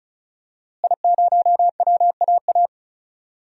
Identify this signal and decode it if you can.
Morse code, heard as a keyed tone, about 36 wpm, 700 Hz. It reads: I0WAA
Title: DXCC-100 Call Signs - 5 Characters - Repeated 2x - 35wpm